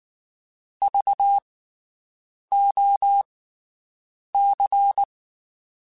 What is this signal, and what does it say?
Morse code, heard as a keyed tone, about 19 wpm, spaced as for 7 wpm, 800 Hz. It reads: VOC
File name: (spaced) / QSOs Spaced-mp3s / WD0BGP 1 @18 sp